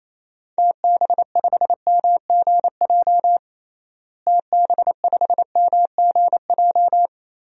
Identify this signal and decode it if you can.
Morse code, heard as a keyed tone, about 27 wpm, 700 Hz. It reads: TB5MGJ TB5MGJ
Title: DXCC-100 Call Signs - 6 Characters - Repeated 3x - 28wpm